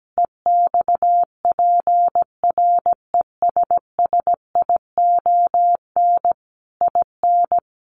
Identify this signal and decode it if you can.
Morse code, heard as a keyed tone, about 17 wpm, 700 Hz. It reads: EXPRESSION IN
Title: Sentences from Top 1200 Words 17wpm